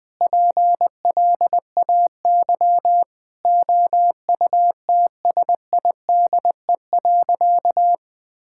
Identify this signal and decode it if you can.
Morse code, heard as a keyed tone, about 20 wpm, 700 Hz. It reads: PLAY OUTSIDE.
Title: Sentences with 3-Word Phrases - 20wpm